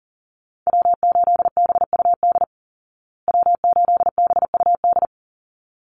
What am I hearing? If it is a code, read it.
Morse code, heard as a keyed tone, about 40 wpm, 700 Hz. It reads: W8BUD W8BUD
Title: Call Signs Repeated 3x - 40wpm